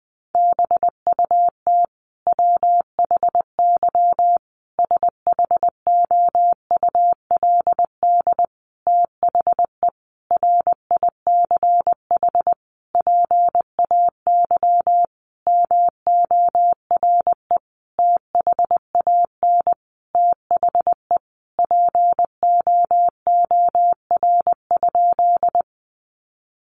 Morse code, heard as a keyed tone, about 20 wpm, 700 Hz. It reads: BUT WHY SHOULD THE RICH PAY MORE THAN THE POOR?
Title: Sentences from Top 1100 Words 20wpm